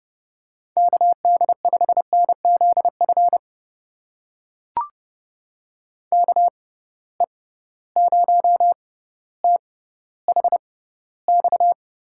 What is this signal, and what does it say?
Morse code, heard as a keyed tone, about 30 wpm, spaced as for 15 wpm, 700 Hz. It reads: KD5NZF KE0THX
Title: USA Call Signs - 6 Characters - 6x Character Spacing - 30wpm